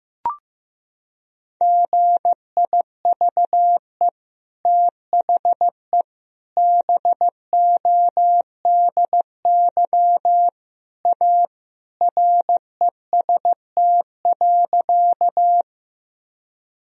Morse code, heard as a keyed tone, about 15 wpm, 700 Hz. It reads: GIVE THE BODY A REST.
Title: Sentences from Top 1000 Words - Review Part 2 - 15wpm